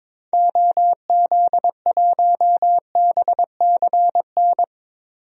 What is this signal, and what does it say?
Morse code, heard as a keyed tone, about 22 wpm, 700 Hz. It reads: OZ1BCN
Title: DXCC-100 Call Signs - 6 Characters - Repeated 3x - 22wpm